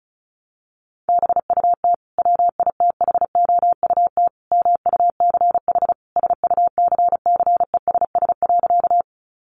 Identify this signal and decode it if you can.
Morse code, heard as a keyed tone, about 34 wpm, 700 Hz. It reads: BUT WITHOUT MUCH SUCCESS.
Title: Sentences from Top 900 Words 35wpm